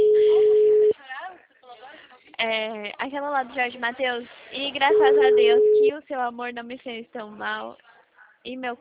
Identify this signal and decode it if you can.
Ringback tone (Europe, 425 Hz), with speech.